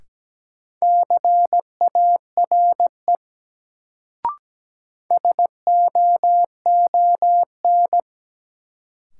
Morse code, heard as a keyed tone, about 17 wpm, 700 Hz. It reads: CARE SOON
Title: Sets of 3 Words - New Word Drill - 17wpm